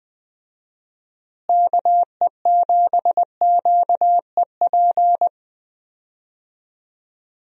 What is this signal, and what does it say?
Morse code, heard as a keyed tone, about 20 wpm, 700 Hz. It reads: KE7QEP
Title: USA Call Signs - 6 Characters - Repeated 3x - 20wpm